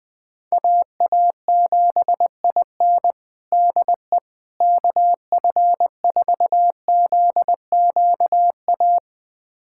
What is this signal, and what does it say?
Morse code, heard as a keyed tone, about 20 wpm, 700 Hz. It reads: AA7IN DE KF4ZQA
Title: USA Callsign DE Callsign 20wpm